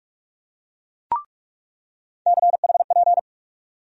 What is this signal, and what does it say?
Morse code, heard as a keyed tone, about 46 wpm, 700 Hz. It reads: CHP
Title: USA Call Sign Postfixes - 3 Letters 45wpm